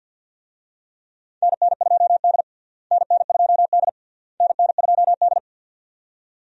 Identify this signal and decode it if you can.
Morse code, heard as a keyed tone, about 51 wpm, 700 Hz. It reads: NN2D NN2D NN2D
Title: USA Call Signs - 4 Characters - Repeated 3x - 50wpm